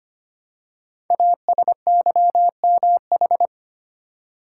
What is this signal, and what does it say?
Morse code, heard as a keyed tone, about 25 wpm, 700 Hz. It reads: ASYMH
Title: Code Groups - 5 Letters - 25wpm